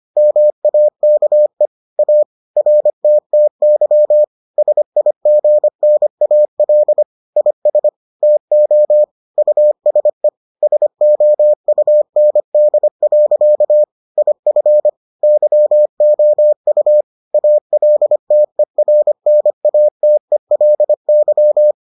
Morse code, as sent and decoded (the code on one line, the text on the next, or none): -- .- -.- . / .- / .-. - - -.-- / ... .. --. -. .- .-.. / .. ... / - --- / ..- ... / ... --- ..- -. -.. .-.-.- / .. ..-. / -.-- --- ..- / .- .-.. - . .-. -. .- - . .-.. -.--
MAKE A RTTY SIGNAL IS TO USE SOUND. IF YOU ALTERNATELY